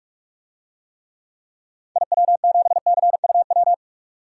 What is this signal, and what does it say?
Morse code, heard as a keyed tone, about 42 wpm, 700 Hz. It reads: IW7CUW
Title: DXCC-100 Call Signs - 6 Characters - 45wpm